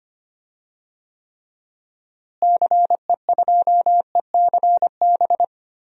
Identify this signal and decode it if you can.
Morse code, heard as a keyed tone, about 25 wpm, 700 Hz. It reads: CE2ECB
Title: DXCC-100 Call Signs - 6 Characters - Repeated 3x - 25wpm